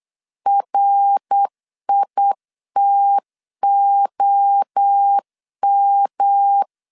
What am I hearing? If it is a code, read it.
Morse code, heard as a keyed tone, about 8 wpm, 800 Hz. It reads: RITOM
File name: CW / DET / 1 3 1500C S B 8wpm 4141